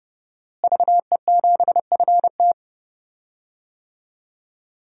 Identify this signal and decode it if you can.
Morse code, heard as a keyed tone, about 30 wpm, 700 Hz. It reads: VE7FT
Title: DXCC-100 Call Signs - 5 Characters - Repeated 2x - 30wpm